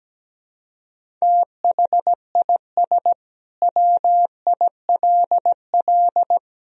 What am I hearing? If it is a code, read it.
Morse code, heard as a keyed tone, about 17 wpm, 700 Hz. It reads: THIS WILL
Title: Sentences from Top 1000 Words - Review Part 1 - 17wpm